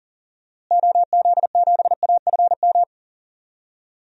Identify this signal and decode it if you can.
Morse code, heard as a keyed tone, about 40 wpm, 700 Hz. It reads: OZ7AFM
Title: DXCC-100 Call Signs - 6 Characters - Repeated 3x - 40wpm